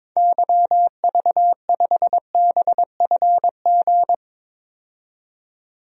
Morse code, heard as a keyed tone, about 22 wpm, 700 Hz. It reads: YV5BFG
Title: DXCC-100 Call Signs - 6 Characters - 22wpm